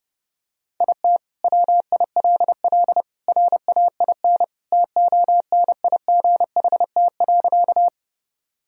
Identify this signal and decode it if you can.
Morse code, heard as a keyed tone, about 32 wpm, 700 Hz. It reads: IT WILL RAIN TONIGHT.
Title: Sentences from Top 1000 Words - Review Part 1 - 30wpm